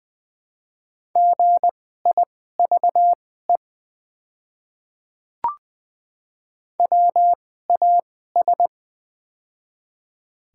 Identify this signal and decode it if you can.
Morse code, heard as a keyed tone, about 20 wpm, spaced as for 15 wpm, 700 Hz. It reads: GIVE WAS
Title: 2x Character Spacing - Top 200 Words - 20wpm